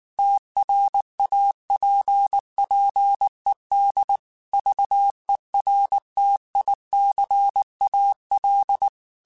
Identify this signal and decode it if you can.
Morse code, heard as a keyed tone, about 19 wpm, 800 Hz. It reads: TRAPPED VERTICAL
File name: QSOs / QSOs-mp3s / kc7sac@18